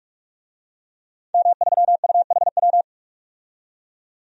Morse code, heard as a keyed tone, about 41 wpm, 700 Hz. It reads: M3UHW
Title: DXCC-100 Call Signs - 5 Characters - 45wpm